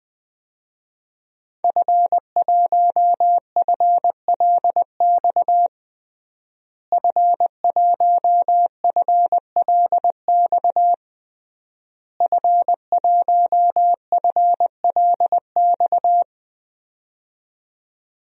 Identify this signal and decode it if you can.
Morse code, heard as a keyed tone, about 20 wpm, 700 Hz. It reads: F1FLX F1FLX F1FLX
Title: DXCC-100 Call Signs - 5 Characters - Repeated 3x - 20wpm